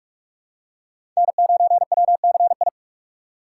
Morse code, heard as a keyed tone, about 46 wpm, 700 Hz. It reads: N9WCI
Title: USA Call Signs - 5 Characters - 4x Character Spacing - 45wpm